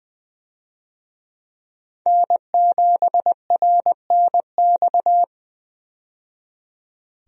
Morse code, heard as a keyed tone, about 20 wpm, 700 Hz. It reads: N7RNX
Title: USA Call Signs - 5 Characters - 20wpm